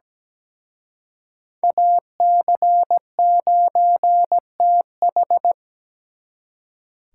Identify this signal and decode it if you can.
Morse code, heard as a keyed tone, about 20 wpm, 700 Hz. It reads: AC9TH
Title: Speed Racing - USA 2x2 Call Signs 17wpm